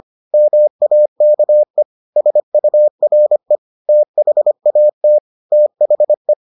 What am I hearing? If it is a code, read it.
Morse code, heard as a keyed tone, about 25 wpm, 600 Hz. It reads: MAKE SURE THAT THE